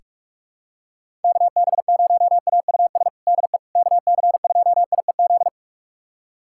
Morse code, heard as a keyed tone, about 46 wpm, 690 Hz. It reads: KB0AUS DE KC2IEZ
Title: USA Callsign DE Callsign 45wpm